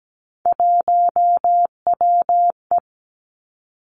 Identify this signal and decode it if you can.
Morse code, heard as a keyed tone, about 17 wpm, 700 Hz. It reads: 1WE
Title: Three Challenging Letter-Number Combinations 17wpm